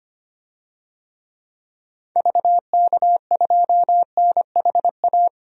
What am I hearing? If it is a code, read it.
Morse code, heard as a keyed tone, about 25 wpm, 700 Hz. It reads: VK2NHA
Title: DXCC-100 Call Signs - 6 Characters - 25wpm